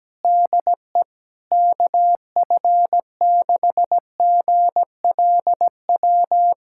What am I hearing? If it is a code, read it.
Morse code, heard as a keyed tone, about 17 wpm, 700 Hz. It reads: DE KF6GLW
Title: SOTA QSO Practice 17wpm